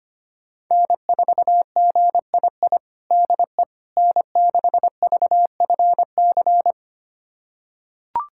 Morse code, heard as a keyed tone, about 25 wpm, 700 Hz. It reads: N4GII DE N6VFC
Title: USA Callsign DE Callsign - 5 Characters - 25wpm